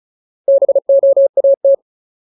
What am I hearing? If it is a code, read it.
Morse code, heard as a keyed tone, about 35 wpm, 550 Hz. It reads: BOAT
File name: Ultimate Word List / 35WPM Ultimate